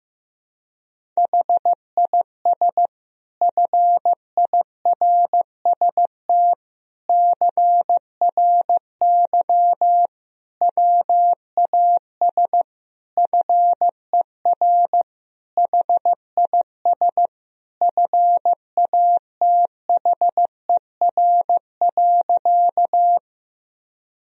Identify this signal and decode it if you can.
Morse code, heard as a keyed tone, about 15 wpm, 700 Hz. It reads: HIS FIRST CRY WAS FER HIS FATHER.
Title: Sentences from Top 500 Words and Abbreviations 15wpm